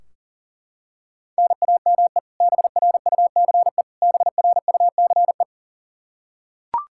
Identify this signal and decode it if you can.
Morse code, heard as a keyed tone, about 40 wpm, 700 Hz. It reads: NAME BRUCE BRUCE
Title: Practice QSO - Exchange 1 - Beginning - 40wpm